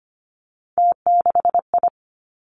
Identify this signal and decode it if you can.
Morse code, heard as a keyed tone, about 25 wpm, 700 Hz. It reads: T6I
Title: Three Challenging Letter-Number Combinations 25wpm